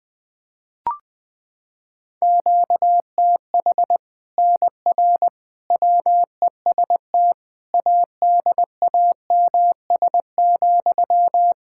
Morse code, heard as a keyed tone, about 20 wpm, 700 Hz. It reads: QTH NR WEST ADAMS,
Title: Practice QSO - Exchange 1 - Beginning - 20wpm